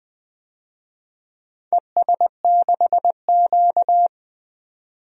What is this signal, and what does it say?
Morse code, heard as a keyed tone, about 20 wpm, 700 Hz. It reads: ES6Q
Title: DXCC-100 Call Signs - 4 Characters - 20wpm